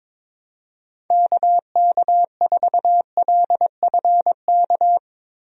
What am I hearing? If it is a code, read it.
Morse code, heard as a keyed tone, about 22 wpm, 700 Hz. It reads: KK4LFK
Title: USA Call Signs - 6 Characters - Repeated 3x - 22wpm